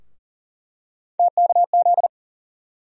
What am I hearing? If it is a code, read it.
Morse code, heard as a keyed tone, about 40 wpm, 700 Hz. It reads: TKZ
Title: USA Call Sign Postfixes - 3 Letters 40wpm